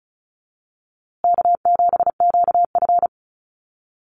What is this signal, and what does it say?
Morse code, heard as a keyed tone, about 34 wpm, 700 Hz. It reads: K7QF
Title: USA Call Signs Repeated 2x 35wpm